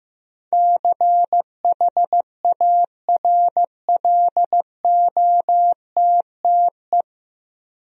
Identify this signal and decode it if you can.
Morse code, heard as a keyed tone, about 15 wpm, 700 Hz. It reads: CHARLOTTE
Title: USA Common Names - 15wpm